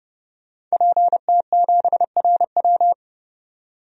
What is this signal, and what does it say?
Morse code, heard as a keyed tone, about 31 wpm, 700 Hz. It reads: PT7RW
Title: DXCC-100 Call Signs - 5 Characters - 30wpm